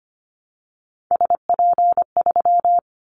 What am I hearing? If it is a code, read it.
Morse code, heard as a keyed tone, about 25 wpm, 700 Hz. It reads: SP3
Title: Callsign Prefixes for Top 25 DXCC - 25wpm